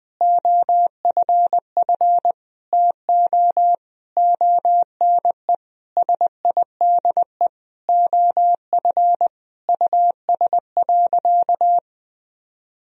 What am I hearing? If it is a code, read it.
Morse code, heard as a keyed tone, about 20 wpm, 700 Hz. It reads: OFF TO ONE SIDE OF US.
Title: Sentences from Top 1000 Words - Review Part 3 - 20wpm